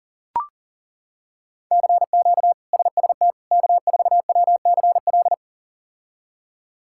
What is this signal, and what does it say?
Morse code, heard as a keyed tone, about 37 wpm, 700 Hz. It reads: CQ SST K4WCL
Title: K1USN SST QSO Practice - 40wpm